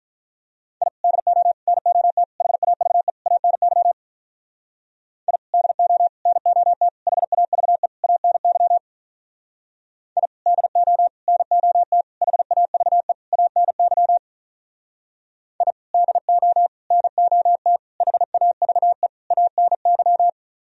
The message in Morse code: I DO NOT HAVE ANY I DO NOT HAVE ANY I DO NOT HAVE ANY I DO NOT HAVE ANY

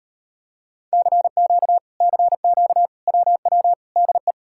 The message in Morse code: CQ CQ WW DE

